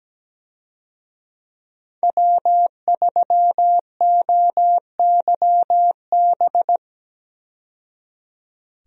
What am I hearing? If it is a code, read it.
Morse code, heard as a keyed tone, about 17 wpm, 700 Hz. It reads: W3OYB
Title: USA Call Signs - 5 Characters - 17wpm